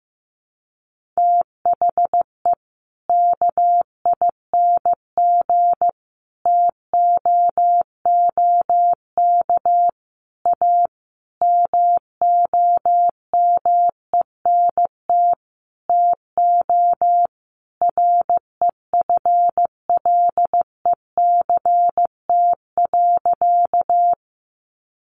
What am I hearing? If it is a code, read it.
Morse code, heard as a keyed tone, about 15 wpm, 700 Hz. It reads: THE KING TOOK A MOMENT TO REFLECT.
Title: Sentences from Top 1500 Words 15wpm